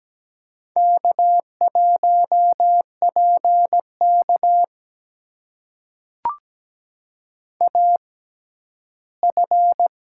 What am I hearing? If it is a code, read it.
Morse code, heard as a keyed tone, about 17 wpm, spaced as for 11 wpm, 700 Hz. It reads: K1PK AF
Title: USA Call Signs - 6x Character Spacing - 17wpm